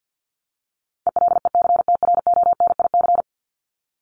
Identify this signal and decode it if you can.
Morse code, heard as a keyed tone, about 50 wpm, 700 Hz. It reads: ELECTRONIC
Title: Top 1001-1250 Words 50wpm